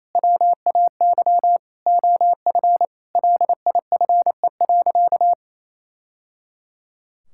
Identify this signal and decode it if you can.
Morse code, heard as a keyed tone, about 28 wpm, 700 Hz. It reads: WAY OF LIFE.
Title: Sentences from Top 1000 Words - Review Part 6 - 28wpm